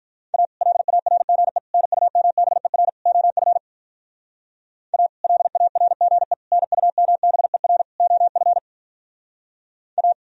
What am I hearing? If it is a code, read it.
Morse code, heard as a keyed tone, about 44 wpm, 700 Hz. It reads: A LARGE NUMBER OF A LARGE NUMBER OF A